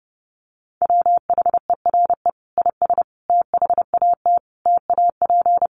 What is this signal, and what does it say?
Morse code, heard as a keyed tone, about 30 wpm, 700 Hz. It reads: WHERE IS THAT TAP
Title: Sentences from Top 1400 Words 30wpm